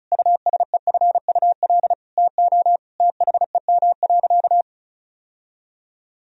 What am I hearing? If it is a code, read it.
Morse code, heard as a keyed tone, about 36 wpm, 700 Hz. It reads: USEFUL TO THEM.
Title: Sentences from Top 1000 Words - Review Part 4 - 35wpm